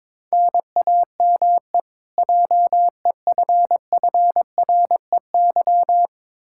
Morse code, heard as a keyed tone, about 22 wpm, 700 Hz. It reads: NAME JEFFREY
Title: Practice QSO - Exchange 1 - Beginning - 22wpm